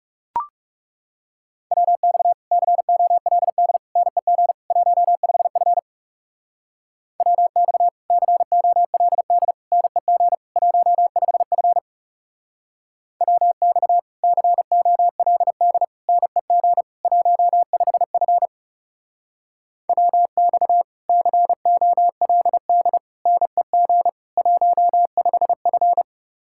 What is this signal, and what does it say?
Morse code, heard as a keyed tone, about 38 wpm, 700 Hz. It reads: WX COLD NEG 15F WX COLD NEG 15F WX COLD NEG 15F WX COLD NEG 15F